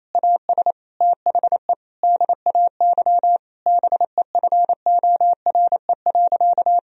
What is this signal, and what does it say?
Morse code, heard as a keyed tone, about 28 wpm, 700 Hz. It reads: AS THE DAY BEFORE.
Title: Sentences from Top 1000 Words - Review Part 1 - 28wpm